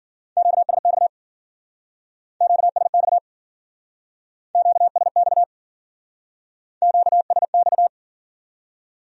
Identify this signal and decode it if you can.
Morse code, heard as a keyed tone, about 53 wpm, 700 Hz. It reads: QSX QSX QSX QSX